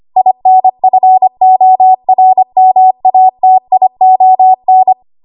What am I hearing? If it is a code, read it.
Morse code, heard as a keyed tone, about 25 wpm, 750 Hz. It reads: INFORMATION
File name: Morse / 25WPM